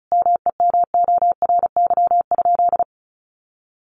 Morse code, heard as a keyed tone, about 34 wpm, 700 Hz. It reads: MEMORY?
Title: Sentences from Top 1300 Words 35wpm